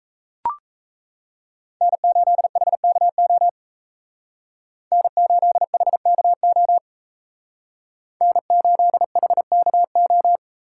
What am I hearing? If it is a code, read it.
Morse code, heard as a keyed tone, about 33 wpm, 700 Hz. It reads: N8HKO N8HKO N8HKO
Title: Speed Racing - USA 1x3 Call Signs 28wpm